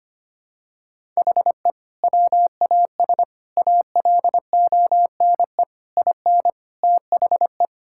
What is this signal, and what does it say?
Morse code, heard as a keyed tone, about 25 wpm, 700 Hz. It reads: HE WAS ALONE IN THE
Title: Sentences from Top 1000 Words - Review Part 6 - 25wpm